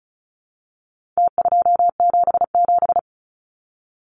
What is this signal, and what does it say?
Morse code, heard as a keyed tone, about 34 wpm, 700 Hz. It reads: T277
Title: Numbers - Contest Style - 4-Digit 35wpm